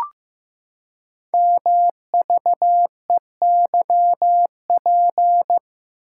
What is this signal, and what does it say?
Morse code, heard as a keyed tone, about 15 wpm, 700 Hz. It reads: MVEYP
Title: Code Groups - 5 Letters - 15wpm